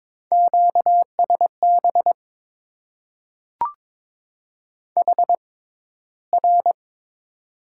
Morse code, heard as a keyed tone, about 22 wpm, spaced as for 14 wpm, 700 Hz. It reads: QSB HR
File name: Top 90 QSO Elements - 6x Character Spacing - 22wpm